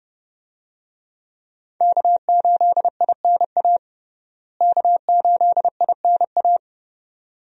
Morse code, heard as a keyed tone, about 30 wpm, 700 Hz. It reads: K8INA K8INA
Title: USA Call Signs - 5 Characters - Repeated 2x - 30wpm